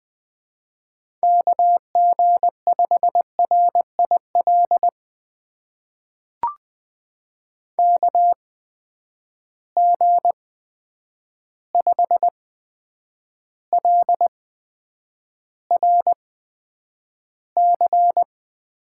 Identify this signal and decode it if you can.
Morse code, heard as a keyed tone, about 20 wpm, spaced as for 9 wpm, 700 Hz. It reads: KG5RIL KG5LRC
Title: USA Call Signs - 6 Characters - 8x Character Spacing - 20wpm